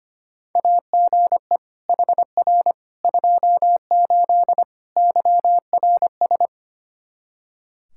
Morse code, heard as a keyed tone, about 25 wpm, 700 Hz. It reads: AGE HR 28 YRS
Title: Practice QSO - Exchange 3 - More Info - 25wpm